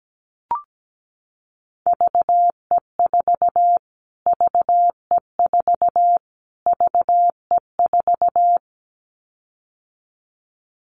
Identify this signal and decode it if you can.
Morse code, heard as a keyed tone, about 17 wpm, 700 Hz. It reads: VE4 VE4 VE4
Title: Callsign Prefixes for Top 25 DXCC - Repeated 3x - 17wpm